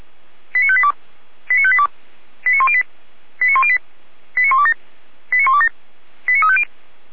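Five-tone selective call (ZVEI), heard as quick runs of five tones, tones of about 70 ms. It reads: 89582 89582 89108 89108 89127 89127 89350
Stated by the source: ZVEI five-tone selective call received via SDR.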